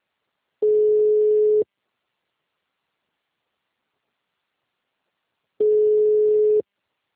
Ringback tone (Europe, 425 Hz).